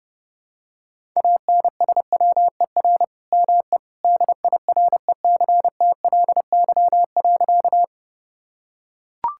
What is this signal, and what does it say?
Morse code, heard as a keyed tone, about 32 wpm, 700 Hz. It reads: ANSWER ME DIRECTLY.
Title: Sentences from Top 1000 Words - Review Part 5 - 30wpm